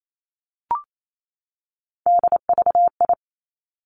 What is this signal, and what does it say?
Morse code, heard as a keyed tone, about 28 wpm, 700 Hz. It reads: DVI
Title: Three Challenging Letter-Number Combinations 28wpm